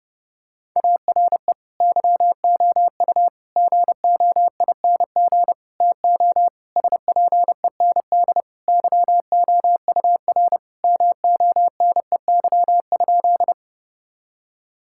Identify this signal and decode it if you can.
Morse code, heard as a keyed tone, about 30 wpm, 700 Hz. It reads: ARE YOU GOING TO SPEND YOUR MONEY?
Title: Sentences from Top 1000 Words - Review Part 5 - 30wpm